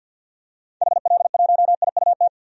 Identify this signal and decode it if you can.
Morse code, heard as a keyed tone, about 52 wpm, 700 Hz. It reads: HL1IUT
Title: DXCC-100 Call Signs - 6 Characters - 50wpm